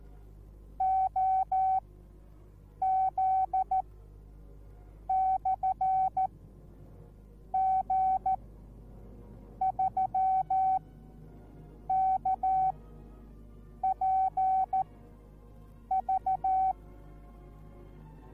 Morse code, heard as a keyed tone, about 14 wpm, spaced as for 6 wpm, 760 Hz. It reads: OZ/G3KPV